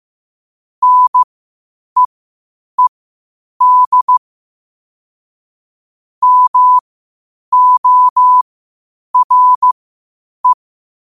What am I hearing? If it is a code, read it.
Morse code, heard as a keyed tone, about 15 wpm, spaced as for 8 wpm, 1000 Hz. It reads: NEED MORE
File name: qotdfiles / 8wpm